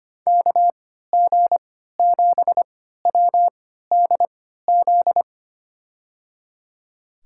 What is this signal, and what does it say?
Morse code, heard as a keyed tone, about 25 wpm, spaced as for 14 wpm, 700 Hz. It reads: KG7WDZ
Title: USA Call Signs - 6 Characters - 3x Character Spacing - 25wpm